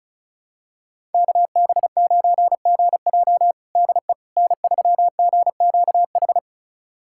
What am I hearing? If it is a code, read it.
Morse code, heard as a keyed tone, about 36 wpm, 700 Hz. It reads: KB9GJ DE N3GQH
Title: USA Callsign DE Callsign - 5 Characters - 35wpm